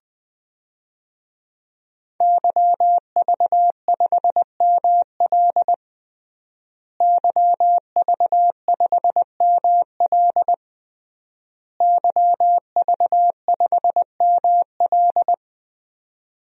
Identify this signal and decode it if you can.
Morse code, heard as a keyed tone, about 20 wpm, 700 Hz. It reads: YV5ML YV5ML YV5ML
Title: DXCC-100 Call Signs - 5 Characters - Repeated 3x - 20wpm